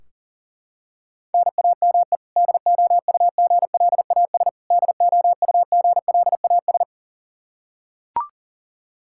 Morse code, heard as a keyed tone, about 40 wpm, 700 Hz. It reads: NAME DOUGLAS DOUGLAS
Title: Practice QSO - Exchange 1 - Beginning - 40wpm